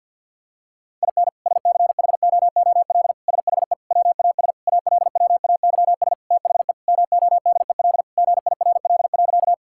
Morse code, heard as a keyed tone, about 51 wpm, 700 Hz. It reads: IN SCHOOL SHE WAS ALWAYS THE MODEL GIRL.
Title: Sentences from Top 1000 Words - Review Part 5 - 50wpm